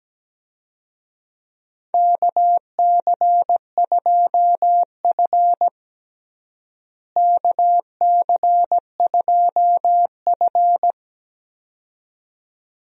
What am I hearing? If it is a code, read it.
Morse code, heard as a keyed tone, about 17 wpm, 700 Hz. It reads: KC2F KC2F
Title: USA Call Signs - 4 Characters - Repeated 2x - 17wpm